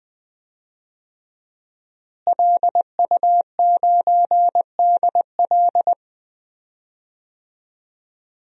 Morse code, heard as a keyed tone, about 20 wpm, 700 Hz. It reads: LU9DL
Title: DXCC-100 Call Signs - 5 Characters - Repeated 2x - 20wpm